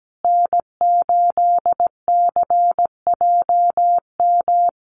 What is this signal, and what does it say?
Morse code, heard as a keyed tone, about 17 wpm, 700 Hz. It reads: N8CJM
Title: USA Call Signs Repeated 3x 17wpm